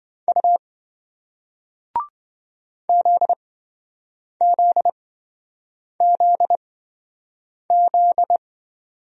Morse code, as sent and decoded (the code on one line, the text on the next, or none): ..- / --.. --.. --.. --..
U ZZZZ